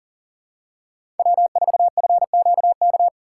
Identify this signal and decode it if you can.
Morse code, heard as a keyed tone, about 40 wpm, 700 Hz. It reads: W4FQK
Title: USA Call Signs - 5 Characters - 2x Character Spacing - 40wpm